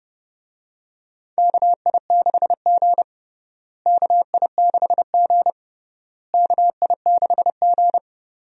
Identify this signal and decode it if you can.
Morse code, heard as a keyed tone, about 30 wpm, 700 Hz. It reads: KI6G KI6G KI6G
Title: USA Call Signs - 4 Characters - Repeated 3x - 30wpm